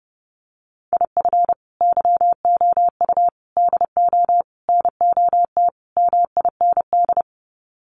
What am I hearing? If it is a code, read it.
Morse code, heard as a keyed tone, about 30 wpm, 700 Hz. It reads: IF YOU DO NOT MIND
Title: Sets of 5 Words 30wpm